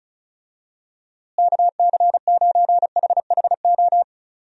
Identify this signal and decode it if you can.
Morse code, heard as a keyed tone, about 36 wpm, 700 Hz. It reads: KC9HHO
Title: USA Call Signs - 6 Characters - Repeated 2x - 35wpm